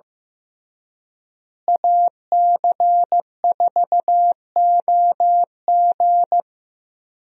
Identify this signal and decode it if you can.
Morse code, heard as a keyed tone, about 16 wpm, 700 Hz. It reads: AC4OG
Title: USA Call Signs - 5 Characters - Repeated 3x - 15wpm